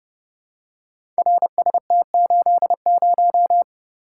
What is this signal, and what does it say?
Morse code, heard as a keyed tone, about 31 wpm, 700 Hz. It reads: RST80
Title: Code Groups - 5 Letter:Numbers - 30wpm